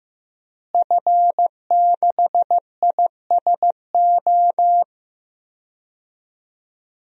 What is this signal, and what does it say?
Morse code, heard as a keyed tone, about 15 wpm, 700 Hz. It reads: F6ISO
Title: DXCC-100 Call Signs - 5 Characters - Repeated 3x - 15wpm